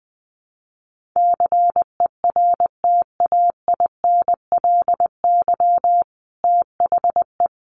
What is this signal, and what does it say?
Morse code, heard as a keyed tone, about 20 wpm, 700 Hz. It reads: CERTAINLY THE
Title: Sentences from Top 1200 Words 20wpm